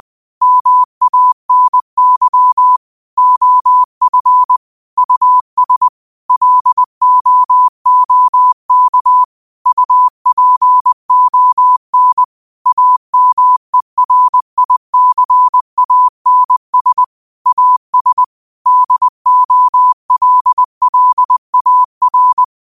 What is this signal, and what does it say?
Morse code, heard as a keyed tone, about 20 wpm, 1000 Hz. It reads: MANY OF US LOOK UPON AMERICANS AS DOLLAR
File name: qotdfiles / 20wpm